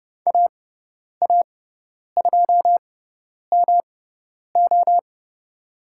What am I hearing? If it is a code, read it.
Morse code, heard as a keyed tone, about 30 wpm, spaced as for 10 wpm, 700 Hz. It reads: AA2MO
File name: Lesson 18g - Letter m - Call Signs 30wpm